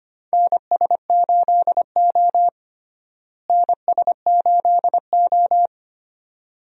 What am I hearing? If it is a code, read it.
Morse code, heard as a keyed tone, about 25 wpm, 700 Hz. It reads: NS8O NS8O
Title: USA Call Signs - 4 Characters - Repeated 2x - 25wpm